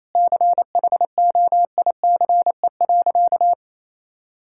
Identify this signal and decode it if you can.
Morse code, heard as a keyed tone, about 28 wpm, 700 Hz. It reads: CHOICE.
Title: Sentences from Top 1000 Words - Review Part 6 - 28wpm